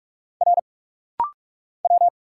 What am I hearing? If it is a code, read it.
Morse code, heard as a keyed tone, about 45 wpm, 700 Hz. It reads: R W
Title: Single Letter-Number Alt - Rapid-Fire 45wpm